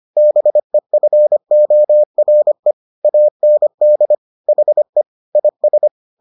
Morse code, heard as a keyed tone, about 25 wpm, 600 Hz. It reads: BEFORE AND HE IS